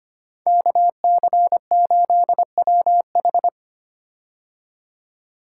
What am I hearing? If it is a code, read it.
Morse code, heard as a keyed tone, about 25 wpm, 700 Hz. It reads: KC8WH
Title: K1USN SST QSO Practice - 25wpm